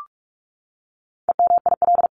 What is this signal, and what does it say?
Morse code, heard as a keyed tone, about 44 wpm, 700 Hz. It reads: EMIL
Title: CWT Exchange 45wpm